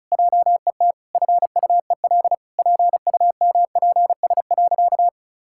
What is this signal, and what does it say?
Morse code, heard as a keyed tone, about 36 wpm, 700 Hz. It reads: JET FUEL PUMPS.
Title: Pangram Challenge v2 - 35wpm